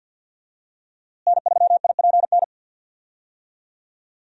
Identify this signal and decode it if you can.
Morse code, heard as a keyed tone, about 53 wpm, 700 Hz. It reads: N3IPN
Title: USA Call Signs - 5 Characters - Repeated 3x - 50wpm